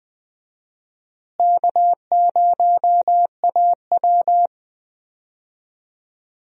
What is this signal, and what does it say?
Morse code, heard as a keyed tone, about 20 wpm, 700 Hz. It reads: K0AW
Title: USA Call Signs - 4 Characters - Repeated 3x - 20wpm